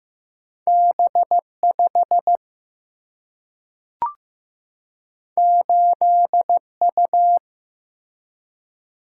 Morse code, heard as a keyed tone, about 15 wpm, 700 Hz. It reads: B5 8U
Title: Single-Letter Single-Number 15wpm